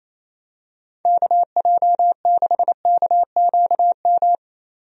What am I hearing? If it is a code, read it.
Morse code, heard as a keyed tone, about 28 wpm, 700 Hz. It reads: KJ6KQM
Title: USA Call Signs - 6 Characters - 28wpm